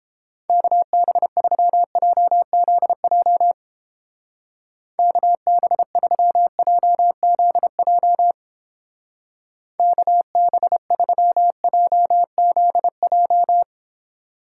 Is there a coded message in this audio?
Yes: Morse code from 0.5 to 13.6 s, heard as a keyed tone, about 29 wpm, 700 Hz. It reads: KB3JZJ KB3JZJ KB3JZJ